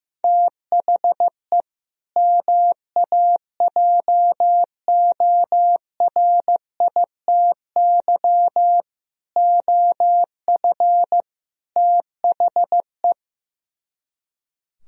Morse code, heard as a keyed tone, about 16 wpm, 700 Hz. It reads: THE MAJORITY OF THE
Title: Speed-Racing - Sets of 4 Words 15wpm